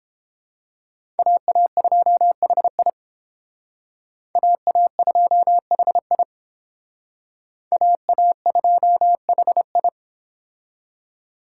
Morse code, heard as a keyed tone, about 29 wpm, 700 Hz. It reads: AA2HI AA2HI AA2HI